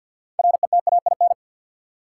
Morse code, heard as a keyed tone, about 48 wpm, 700 Hz. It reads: RETAIN
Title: Six Letter Words - 50wpm